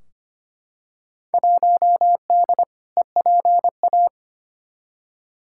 Morse code, heard as a keyed tone, about 25 wpm, 700 Hz. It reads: 1D EPA
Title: Field Day Exchanges - 25wpm